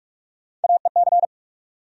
Morse code, heard as a keyed tone, about 42 wpm, 700 Hz. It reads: AEC
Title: USA Call Sign Postfixes - 3 Letters 45wpm